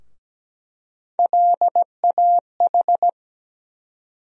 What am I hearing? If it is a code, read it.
Morse code, heard as a keyed tone, about 17 wpm, 700 Hz. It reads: LAH
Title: USA Call Sign Postfixes - 3 Letters 17wpm